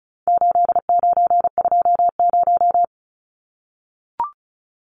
Morse code, heard as a keyed tone, about 34 wpm, 700 Hz. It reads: 8920
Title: Numbers - 4-Digit 35wpm